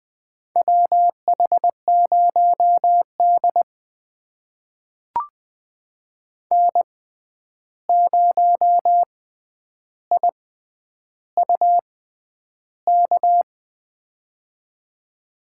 Morse code, heard as a keyed tone, about 20 wpm, spaced as for 10 wpm, 700 Hz. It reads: WH0D N0IUK
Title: USA Call Signs - 6x Character Spacing - 20wpm